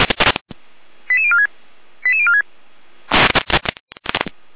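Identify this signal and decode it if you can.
Five-tone selective call (ZVEI), heard as quick runs of five tones, tones of about 70 ms. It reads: 88046 88046
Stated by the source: ZVEI five-tone call received by SDR.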